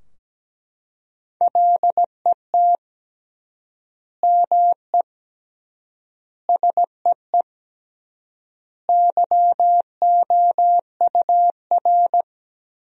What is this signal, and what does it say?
Morse code, heard as a keyed tone, about 17 wpm, 700 Hz. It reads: LET ME SEE YOUR
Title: 3x Word-Spacing - Sentences from Top 500 Words - 17wpm